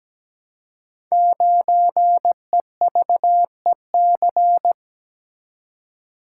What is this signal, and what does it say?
Morse code, heard as a keyed tone, about 17 wpm, 700 Hz. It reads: 9EVEC
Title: Code Groups - 5 Letter:Numbers - 17wpm